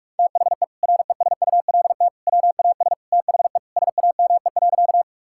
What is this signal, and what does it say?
Morse code, heard as a keyed tone, about 46 wpm, 700 Hz. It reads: THE RESULT WAS THE SAME.
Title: Sentences from Top 1000 Words - Review Part 5 - 45wpm